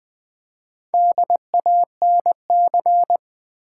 Morse code, heard as a keyed tone, about 20 wpm, 700 Hz. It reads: DANC
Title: Build A Word - 1 Letter At A Time - 5-Letter Words - 20wpm